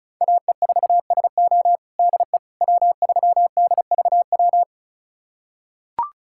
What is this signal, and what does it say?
Morse code, heard as a keyed tone, about 34 wpm, 700 Hz. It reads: AE4SO DE W3DVW
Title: USA Callsign DE Callsign - 5 Characters - 35wpm